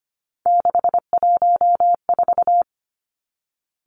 Morse code, heard as a keyed tone, about 25 wpm, 700 Hz. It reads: 614
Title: Three Challenging Letter-Number Combinations 25wpm